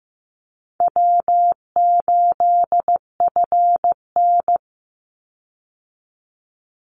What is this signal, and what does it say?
Morse code, heard as a keyed tone, about 15 wpm, 700 Hz. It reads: W8FN
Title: Call Signs Repeated 2x - 15wpm